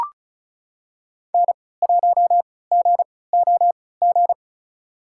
Morse code, heard as a keyed tone, about 36 wpm, spaced as for 20 wpm, 700 Hz. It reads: N1GOG
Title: USA Call Signs - 3x Character Spacing - 35wpm